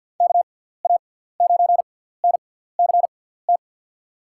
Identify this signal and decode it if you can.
Morse code, heard as a keyed tone, about 52 wpm, spaced as for 17 wpm, 700 Hz. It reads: KA9NCT
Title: USA Call Signs - 6 Characters - 6x Character Spacing - 50wpm